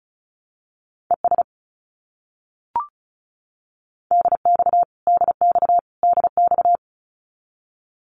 Morse code, heard as a keyed tone, about 34 wpm, 700 Hz. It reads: ES DX DX DX
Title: Abbreviations Repeated 3x - 35wpm